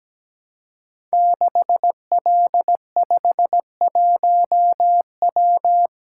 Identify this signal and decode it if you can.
Morse code, heard as a keyed tone, about 17 wpm, 700 Hz. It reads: 6L51W
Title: Code Groups - 5 Letter:Numbers - 17wpm